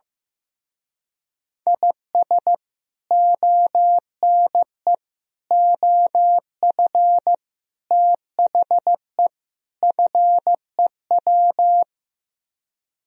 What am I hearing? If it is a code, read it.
Morse code, heard as a keyed tone, about 15 wpm, 700 Hz. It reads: IS ONE OF THE FEW
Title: Speed-Racing - Sets of 5 Words 15wpm